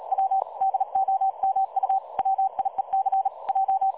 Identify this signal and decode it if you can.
Morse code, heard as a keyed tone, about 39 wpm, 770 Hz. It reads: W DOMU WIEC J
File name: dlazaawansowanych / nagraniazpasma / websdr recording 3565.0kHz